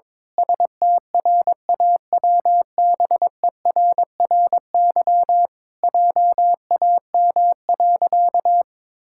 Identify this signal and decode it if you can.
Morse code, heard as a keyed tone, about 21 wpm, 700 Hz. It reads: STRAWBERRY JAM.
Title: Pangram Challenge v2 - 22wpm